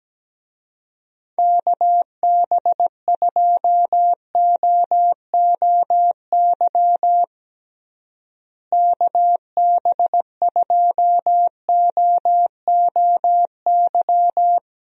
Morse code, heard as a keyed tone, about 17 wpm, 700 Hz. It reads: KB2OOY KB2OOY
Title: USA Call Signs - 6 Characters - Repeated 2x - 17wpm